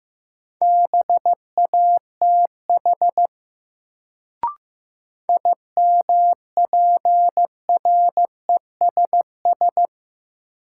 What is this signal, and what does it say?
Morse code, heard as a keyed tone, about 15 wpm, 700 Hz. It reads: BATH IMPRESS
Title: Sentences from Top 1600 Words - New Word Drill Alt - 15wpm